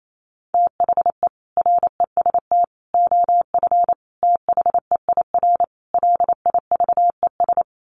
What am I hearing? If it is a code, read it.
Morse code, heard as a keyed tone, about 28 wpm, 700 Hz. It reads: THE REST OF THEIR LIVES
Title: Sets of 5 Words Encore 28wpm